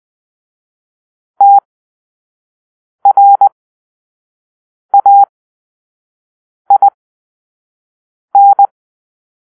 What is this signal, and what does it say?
Morse code, heard as a keyed tone, about 20 wpm, spaced as for 5 wpm, 800 Hz. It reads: TRAIN
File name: arrl / Grp-2 Drill-4